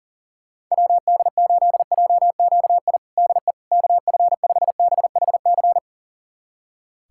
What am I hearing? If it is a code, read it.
Morse code, heard as a keyed tone, about 40 wpm, 700 Hz. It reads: WD8JQI DE KF5BHC
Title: USA Callsign DE Callsign 40wpm